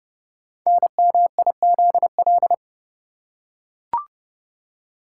Morse code, heard as a keyed tone, about 30 wpm, 700 Hz. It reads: NMIZL
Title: Code Groups - 5 Letters - 30wpm